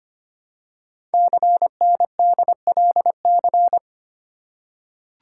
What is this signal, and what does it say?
Morse code, heard as a keyed tone, about 25 wpm, 700 Hz. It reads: CNDLC
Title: Code Groups - 5 Letters - 25wpm